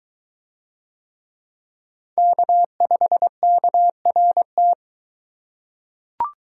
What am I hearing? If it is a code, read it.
Morse code, heard as a keyed tone, about 22 wpm, 700 Hz. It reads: K5KRT